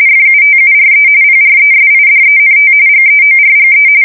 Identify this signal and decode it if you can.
RTTY, about 45.45 baud. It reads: FUTURE PRESIDENT GEORGE
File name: fsk170 / USS-New-Jersey